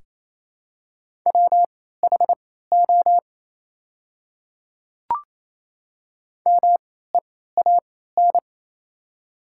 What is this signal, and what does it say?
Morse code, heard as a keyed tone, about 28 wpm, spaced as for 16 wpm, 700 Hz. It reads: WHO MEAN
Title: 3x Character Spacing - Top 200 Words - 28wpm